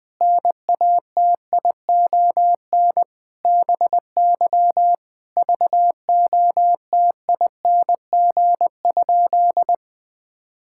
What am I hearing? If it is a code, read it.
Morse code, heard as a keyed tone, about 19 wpm, 700 Hz. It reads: NATION BY VOTING?
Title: Sentences with 9-Word Phrases - Spoken Beginning - 20wpm